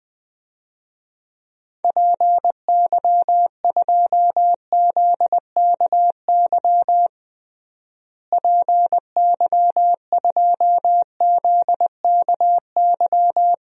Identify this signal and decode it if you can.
Morse code, heard as a keyed tone, about 21 wpm, 700 Hz. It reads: PY2ZKY PY2ZKY
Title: DXCC-100 Call Signs - 6 Characters - Repeated 3x - 20wpm